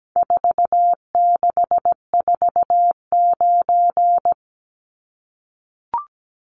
Morse code, heard as a keyed tone, about 17 wpm, 700 Hz. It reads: 4649
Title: Numbers - 4-Digit 17wpm